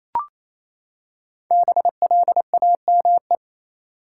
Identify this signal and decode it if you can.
Morse code, heard as a keyed tone, about 28 wpm, 700 Hz. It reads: BLAME
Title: Five Letter Words - 28wpm